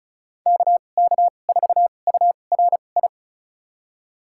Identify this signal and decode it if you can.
Morse code, heard as a keyed tone, about 36 wpm, spaced as for 25 wpm, 700 Hz. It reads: KK4URI
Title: USA Call Signs - 6 Characters - 2x Character Spacing - 35wpm